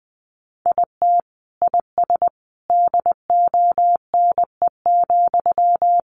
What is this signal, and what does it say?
Morse code, heard as a keyed tone, about 20 wpm, 700 Hz. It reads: IT IS DONE,
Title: Sentences from Top 600 Words 20wpm